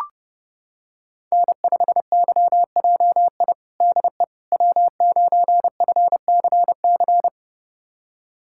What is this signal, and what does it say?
Morse code, heard as a keyed tone, about 30 wpm, 700 Hz. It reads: N5YJI DE W9FCC
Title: USA Callsign DE Callsign - 5 Characters - 30wpm